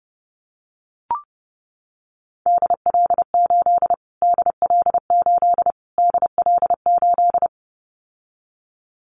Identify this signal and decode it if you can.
Morse code, heard as a keyed tone, about 30 wpm, 700 Hz. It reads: DL8 DL8 DL8
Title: Callsign Prefixes for Top 25 DXCC - Repeated 3x - 30wpm